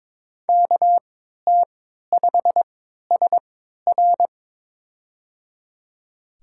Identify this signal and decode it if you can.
Morse code, heard as a keyed tone, about 22 wpm, spaced as for 13 wpm, 700 Hz. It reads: KT5SR
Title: USA Call Signs - 5 Characters - 3x Character Spacing - 22wpm